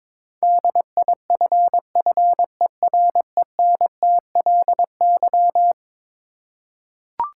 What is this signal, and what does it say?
Morse code, heard as a keyed tone, about 22 wpm, 700 Hz. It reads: DIFFERENTLY
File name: practice / 22wpm / Eleven Letter Words - 22wpm